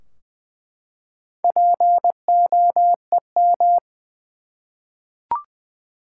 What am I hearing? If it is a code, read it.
Morse code, heard as a keyed tone, about 20 wpm, 700 Hz. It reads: POEM
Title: Four Letter Words - 20wpm